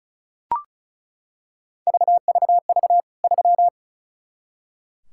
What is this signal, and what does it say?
Morse code, heard as a keyed tone, about 36 wpm, 700 Hz. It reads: VVV 3
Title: K1USN SST QSO Practice 35wpm